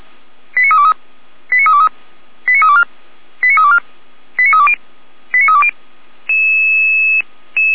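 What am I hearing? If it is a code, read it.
Five-tone selective call (ZVEI), heard as quick runs of five tones, tones of about 70 ms. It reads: 89323 89323 89324 89324 89320 89320
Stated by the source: ZVEI five-tone selective call received on an SDR.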